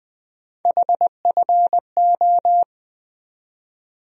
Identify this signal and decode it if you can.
Morse code, heard as a keyed tone, about 20 wpm, 700 Hz. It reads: HFO
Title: USA Call Sign Postfixes - 3 Letters 20wpm